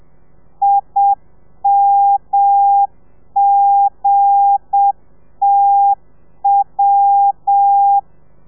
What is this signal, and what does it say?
Morse code, heard as a keyed tone, about 7 wpm, 800 Hz. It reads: IMGTW
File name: telegrafia / i2tfj / Lez8 35